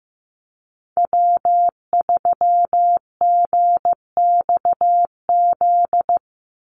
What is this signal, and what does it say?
Morse code, heard as a keyed tone, about 15 wpm, 700 Hz. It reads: W3GXZ
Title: USA Call Signs 15wpm